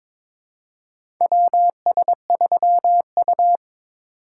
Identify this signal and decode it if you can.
Morse code, heard as a keyed tone, about 22 wpm, 700 Hz. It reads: WS3U
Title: USA Call Signs - 4 Characters - 3x Character Spacing - 22wpm